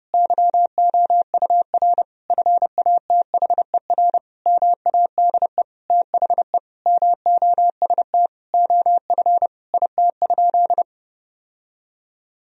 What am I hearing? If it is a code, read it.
Morse code, heard as a keyed tone, about 30 wpm, 700 Hz. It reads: YOUR FATHER MADE THE MOST OF IT?
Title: Sentences from Top 1000 Words - Review Part 1 - 30wpm